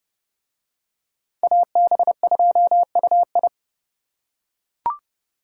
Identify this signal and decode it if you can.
Morse code, heard as a keyed tone, about 29 wpm, 700 Hz. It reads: AB2UI